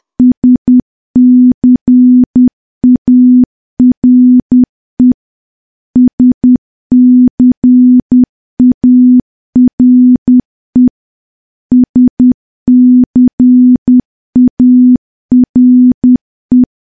Morse code, heard as a keyed tone, about 10 wpm, 250 Hz. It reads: SCARE SCARE SCARE
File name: morse / morse-training-10WPM